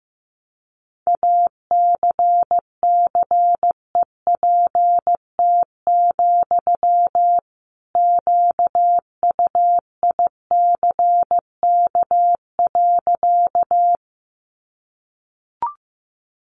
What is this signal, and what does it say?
Morse code, heard as a keyed tone, about 15 wpm, 700 Hz. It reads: ACCEPT, QUICK.
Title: Sentences from Top 1300 Words 15wpm